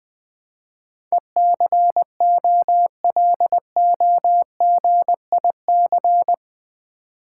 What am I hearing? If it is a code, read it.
Morse code, heard as a keyed tone, about 20 wpm, 700 Hz. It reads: ECOLOGIC
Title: Build A Word - 4 Letters At A Time - 10-Letter Words - 20wpm